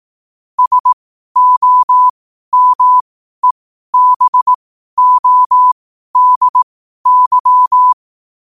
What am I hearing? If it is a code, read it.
Morse code, heard as a keyed tone, about 18 wpm, spaced as for 12 wpm, 1000 Hz. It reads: SOMEBODY
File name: qotdfiles / 12wpm